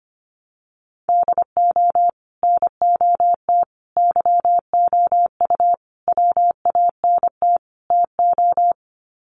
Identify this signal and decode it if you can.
Morse code, heard as a keyed tone, about 25 wpm, 700 Hz. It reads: DO NOT YOU WANT TO
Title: Sets of 5 Words Encore2 25wpm